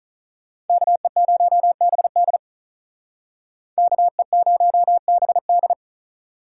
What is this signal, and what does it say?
Morse code, heard as a keyed tone, about 46 wpm, 700 Hz. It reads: KE0BD KE0BD